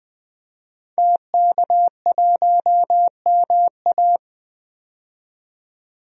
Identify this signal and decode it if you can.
Morse code, heard as a keyed tone, about 20 wpm, 700 Hz. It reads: TK1MA
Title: DXCC-100 Call Signs - 5 Characters - 20wpm